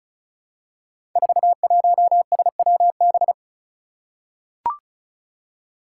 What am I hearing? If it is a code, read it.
Morse code, heard as a keyed tone, about 36 wpm, 700 Hz. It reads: 41SWB
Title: Five Challenging Letter-Number Combinations 35wpm